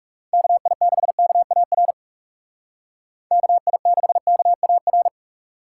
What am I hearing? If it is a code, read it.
Morse code, heard as a keyed tone, about 37 wpm, 700 Hz. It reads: KI6KAR KI6KAR
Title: Speed Racing - USA 2x3 Call Signs 30wpm